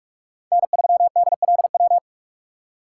Morse code, heard as a keyed tone, about 41 wpm, 700 Hz. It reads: N3DLW